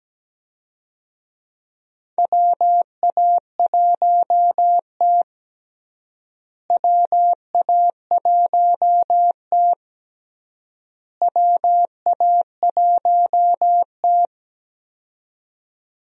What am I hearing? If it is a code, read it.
Morse code, heard as a keyed tone, about 17 wpm, 700 Hz. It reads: WA1T WA1T WA1T
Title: USA Call Signs - 4 Characters - Repeated 3x - 17wpm